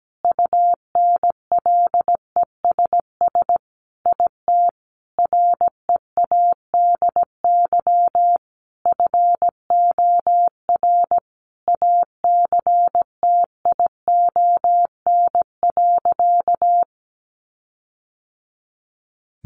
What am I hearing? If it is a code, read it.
Morse code, heard as a keyed tone, about 17 wpm, 700 Hz. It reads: UNLESS IT READY FOR ACTION.
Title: Sentences from Top 1300 Words 17wpm